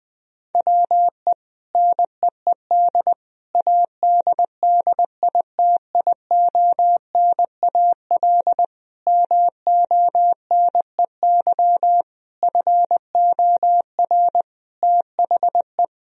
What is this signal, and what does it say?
Morse code, heard as a keyed tone, about 20 wpm, 700 Hz. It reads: WE NEED ADDITIONAL MONEY FOR THE
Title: Sentences from Top 1000 Words - Review Part 4 - 20wpm